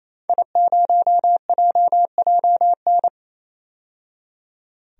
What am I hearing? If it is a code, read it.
Morse code, heard as a keyed tone, about 27 wpm, 700 Hz. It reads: I0JJN
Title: DXCC-100 Call Signs - 5 Characters - 28wpm